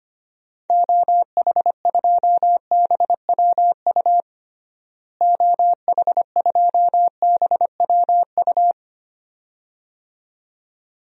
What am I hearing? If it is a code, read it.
Morse code, heard as a keyed tone, about 25 wpm, 700 Hz. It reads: OH2BWU OH2BWU
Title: DXCC-100 Call Signs - 6 Characters - Repeated 2x - 25wpm